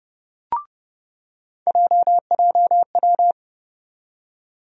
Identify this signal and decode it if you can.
Morse code, heard as a keyed tone, about 30 wpm, 700 Hz. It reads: JJW
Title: USA Call Sign Postfixes - 3 Letters 30wpm